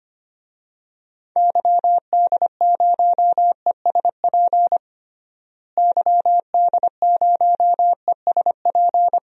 Morse code, heard as a keyed tone, about 25 wpm, 700 Hz. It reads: YD0ESP YD0ESP
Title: DXCC-100 Call Signs - 6 Characters - Repeated 2x - 25wpm